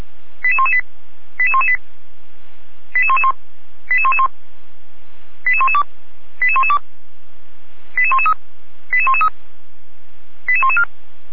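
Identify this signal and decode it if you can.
Five-tone selective call (ZVEI), heard as quick runs of five tones, tones of about 80 ms. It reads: 88118 88118 88111 88111 88112 88112 88113 88113 88114
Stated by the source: ZVEI five-tone selective call received by SDR, from a Feuerwehr Orlamünde fire brigade server.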